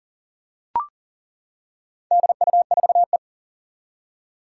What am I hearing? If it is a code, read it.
Morse code, heard as a keyed tone, about 38 wpm, 700 Hz. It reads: DU4E
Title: Four Challenging Letter-Number Combinations 40wpm